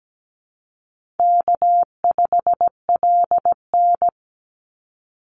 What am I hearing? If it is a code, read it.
Morse code, heard as a keyed tone, about 17 wpm, 700 Hz. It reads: K5LN
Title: Call Signs - 17wpm